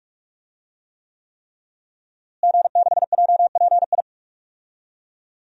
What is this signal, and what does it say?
Morse code, heard as a keyed tone, about 42 wpm, 700 Hz. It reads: G6JPI
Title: DXCC-100 Call Signs - 5 Characters - Repeated 3x - 45wpm